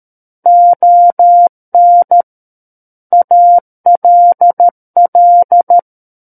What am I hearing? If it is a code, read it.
Morse code, heard as a keyed tone, about 13 wpm, 700 Hz. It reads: ON ALL
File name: Book One / 04-Chapter 4 - 13 WPM